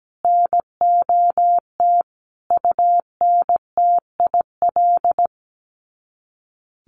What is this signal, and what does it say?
Morse code, heard as a keyed tone, about 17 wpm, 700 Hz. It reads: NOT UNTIL
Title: Sets of 5 Words Encore2 17wpm